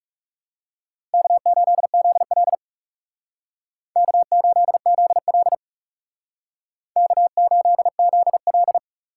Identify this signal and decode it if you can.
Morse code, heard as a keyed tone, about 40 wpm, 700 Hz. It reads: K8ZL K8ZL K8ZL